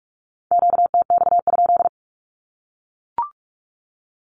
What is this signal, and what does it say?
Morse code, heard as a keyed tone, about 44 wpm, 700 Hz. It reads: QTX?
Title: Q-Codes 45wpm